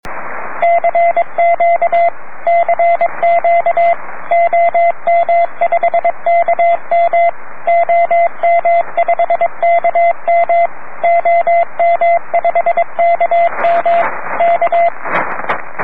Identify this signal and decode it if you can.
Morse code, heard as a keyed tone, about 22 wpm, 680 Hz, with radio noise. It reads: CQ CQ OM5KM OM5KM OM5KM K
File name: om3pv 70mhz / zaznam / om5km